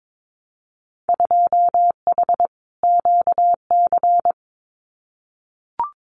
Morse code, heard as a keyed tone, about 22 wpm, 700 Hz. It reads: 2H QC
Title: Winter Field Day Exchanges Encore 22wpm